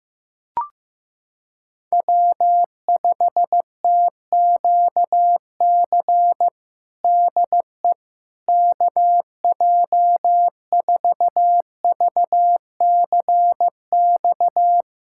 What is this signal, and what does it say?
Morse code, heard as a keyed tone, about 15 wpm, 700 Hz. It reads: W5TQC DE KJ4VCX
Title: USA Callsign DE Callsign 15wpm